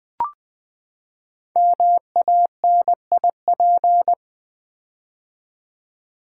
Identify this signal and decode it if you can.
Morse code, heard as a keyed tone, about 20 wpm, 700 Hz. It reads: MANIP
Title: Build A Word - 5 Letters At A Time - 12-Letter Words - 20wpm